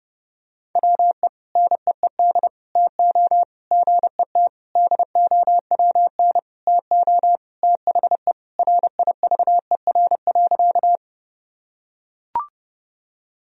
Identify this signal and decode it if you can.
Morse code, heard as a keyed tone, about 30 wpm, 700 Hz. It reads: WE NEED TO GET DOWN TO THE RIVER.
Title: Sentences from Top 1000 Words - Review Part 1 - 30wpm